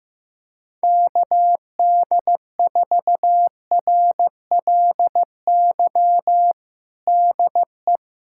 Morse code, heard as a keyed tone, about 15 wpm, 700 Hz. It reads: KD4RLY DE
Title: USA Callsign DE Callsign 15wpm